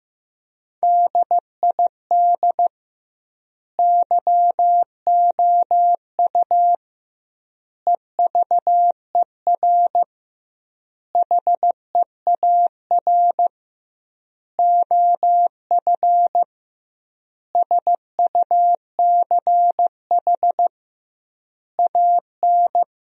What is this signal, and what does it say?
Morse code, heard as a keyed tone, about 15 wpm, 700 Hz. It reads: DID YOU EVER HEAR OF SUCH AN
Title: 2x Word-Spacing - Sentences from Top 400 Words - 15wpm